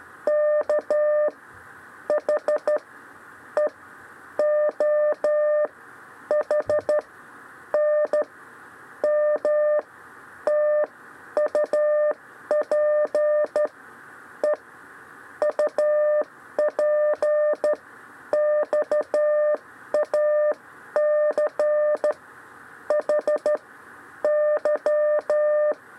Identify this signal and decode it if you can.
Morse code, heard as a keyed tone, about 13 wpm, spaced as for 11 wpm, 600 Hz. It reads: K H E O H N M TUP E UP XAC H Y
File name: 6 lezione cw 2